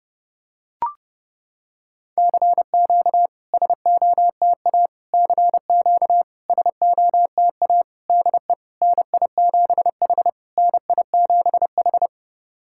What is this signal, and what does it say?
Morse code, heard as a keyed tone, about 31 wpm, 690 Hz. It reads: CQ SOTA CQ SOTA DE NI7H NI7H
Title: SOTA QSO Practice 30wpm